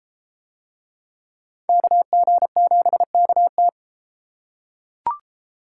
Morse code, heard as a keyed tone, about 30 wpm, 700 Hz. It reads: KG7KT